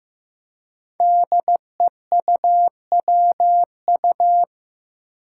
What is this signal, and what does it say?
Morse code, heard as a keyed tone, about 15 wpm, 700 Hz. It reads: DEUWU
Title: Five Challenging Letter-Number Combinations 15wpm